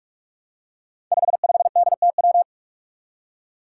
Morse code, heard as a keyed tone, about 44 wpm, 700 Hz. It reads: 55DTW
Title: Five Challenging Letter-Number Combinations 45wpm